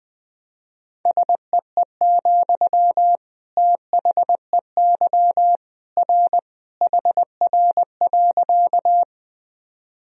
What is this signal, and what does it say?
Morse code, heard as a keyed tone, about 21 wpm, 700 Hz. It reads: SEE, THEY R HR.
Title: Sentences from Top 200 Words and Abbreviations 20wpm